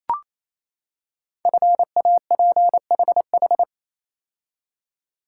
Morse code, heard as a keyed tone, about 27 wpm, 700 Hz. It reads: FAPHH
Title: Code Groups - 5 Letters - 28wpm